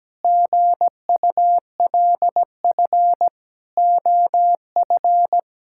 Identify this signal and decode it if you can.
Morse code, heard as a keyed tone, about 17 wpm, 700 Hz. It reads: GULF OF
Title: Pangram Challenge v2 - 17wpm